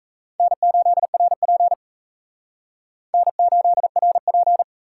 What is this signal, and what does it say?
Morse code, heard as a keyed tone, about 42 wpm, 700 Hz. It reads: N8RP N8RP